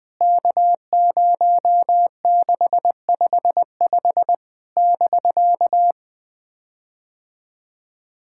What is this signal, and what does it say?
Morse code, heard as a keyed tone, about 20 wpm, 700 Hz. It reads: K0655 <BK>
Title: POTA QSO Practice - 20wpm